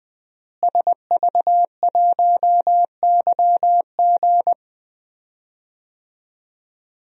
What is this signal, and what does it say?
Morse code, heard as a keyed tone, about 20 wpm, 700 Hz. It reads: SV1YG
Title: DXCC-100 Call Signs - 5 Characters - Repeated 2x - 20wpm